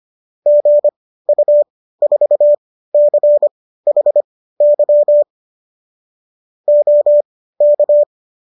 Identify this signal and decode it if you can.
Morse code, heard as a keyed tone, about 25 wpm, spaced as for 15 wpm, 590 Hz. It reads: GU4CHY OK